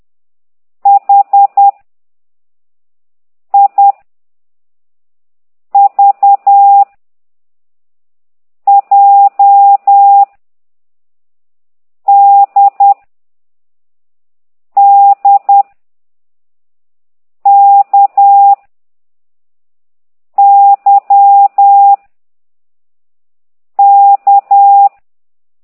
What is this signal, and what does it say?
Morse code, heard as a keyed tone, about 10 wpm, 800 Hz. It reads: H I V J D D K Y K
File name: exerc7